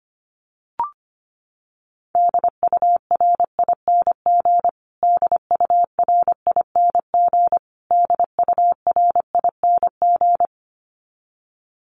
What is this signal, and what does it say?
Morse code, heard as a keyed tone, about 25 wpm, 700 Hz. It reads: DURING DURING DURING
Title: Top 500 Words Repeated 25wpm